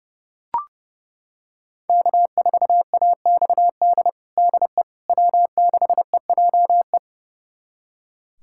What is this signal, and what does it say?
Morse code, heard as a keyed tone, about 31 wpm, 700 Hz. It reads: K4AXD DE W6EJE
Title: USA Callsign DE Callsign 30wpm